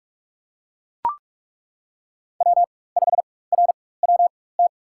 Morse code, heard as a keyed tone, about 41 wpm, spaced as for 20 wpm, 700 Hz. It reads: W5RWT
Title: USA Call Signs - 4x Character Spacing - 45wpm